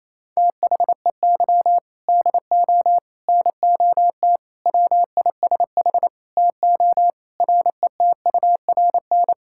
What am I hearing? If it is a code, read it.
Morse code, heard as a keyed tone, about 28 wpm, 700 Hz. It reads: THEY DO NOT WISH TO RETURN
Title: Sentences from Top 1600 Words 28wpm